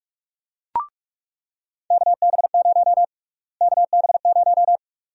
Morse code, heard as a keyed tone, about 42 wpm, 700 Hz. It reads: KB0 KB0
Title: USA Call Sign Prefixes - Repeated 2x - 45wpm